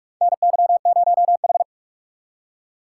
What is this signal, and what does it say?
Morse code, heard as a keyed tone, about 43 wpm, 700 Hz. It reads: NY0H
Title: USA Call Signs - 4 Characters - 8x Character Spacing - 45wpm